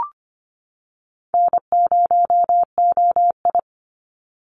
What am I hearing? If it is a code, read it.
Morse code, heard as a keyed tone, about 25 wpm, 700 Hz. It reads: N0OI
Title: NA SOTA Chaser Callsigns 25wpm